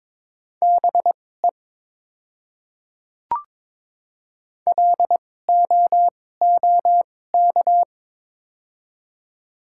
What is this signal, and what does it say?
Morse code, heard as a keyed tone, about 22 wpm, spaced as for 16 wpm, 700 Hz. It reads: BE LOOK
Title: 2x Character Spacing - Top 100 Words - 22wpm